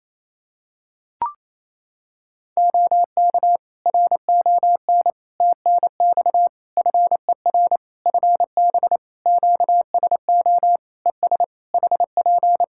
Morse code, heard as a keyed tone, about 28 wpm, 700 Hz. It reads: OK RON TNX FER FB QSO ES HP
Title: Practice QSO - Exchange 4 - Ending - 28wpm